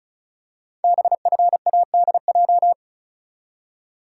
Morse code, heard as a keyed tone, about 36 wpm, 700 Hz. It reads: BFADJ
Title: Code Groups - 5 Letters - 35wpm